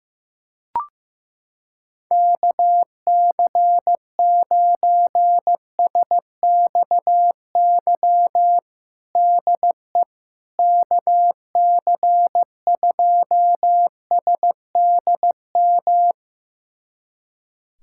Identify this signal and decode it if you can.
Morse code, heard as a keyed tone, about 15 wpm, 700 Hz. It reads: KC9SXY DE KC2SDM
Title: USA Callsign DE Callsign - 6 Characters - 15wpm